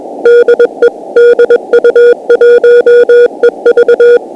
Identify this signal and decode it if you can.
Morse code, heard as a keyed tone, about 21 wpm, 480 Hz. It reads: DE DU1EV
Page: DU1EV/b Beacon 50.008 MHz Operated by Eddie DU1EV and located in PK04, QUEZON CITY, PHILIPPINES